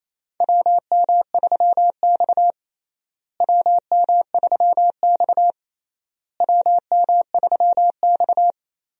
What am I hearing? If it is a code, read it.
Morse code, heard as a keyed tone, about 27 wpm, 700 Hz. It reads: WM3X WM3X WM3X
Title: USA Call Signs - 4 Characters - Repeated 3x - 28wpm